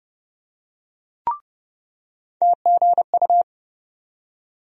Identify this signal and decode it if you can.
Morse code, heard as a keyed tone, about 30 wpm, 700 Hz. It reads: TGU
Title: USA Call Sign Postfixes - 3 Letters 30wpm